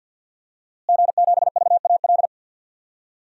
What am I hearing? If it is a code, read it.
Morse code, heard as a keyed tone, about 52 wpm, 700 Hz. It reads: G7VAL
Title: DXCC-100 Call Signs - 5 Characters - Repeated 2x - 50wpm